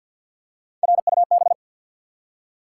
Morse code, heard as a keyed tone, about 48 wpm, 700 Hz. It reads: RUB
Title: Three Letter Words - 50wpm